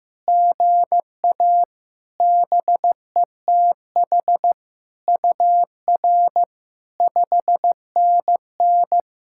Morse code, heard as a keyed tone, about 15 wpm, 700 Hz. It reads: GA BETH UR 5NN
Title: SOTA QSO Practice 15wpm